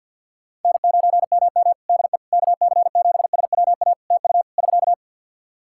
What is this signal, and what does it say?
Morse code, heard as a keyed tone, about 53 wpm, 700 Hz. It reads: N9MM DE KC7SWA TU <SK>
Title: Practice QSO - Exchange 4 - Ending - 50wpm